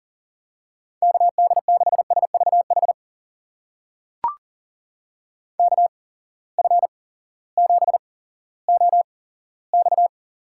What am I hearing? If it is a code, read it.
Morse code, heard as a keyed tone, about 40 wpm, spaced as for 18 wpm, 700 Hz. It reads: KD6SVH KF7OX
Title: USA Call Signs - 6 Characters - 8x Character Spacing - 40wpm